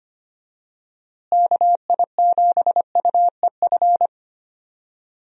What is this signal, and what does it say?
Morse code, heard as a keyed tone, about 25 wpm, 700 Hz. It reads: KI7UEF
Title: USA Call Signs - 6 Characters - Repeated 3x - 25wpm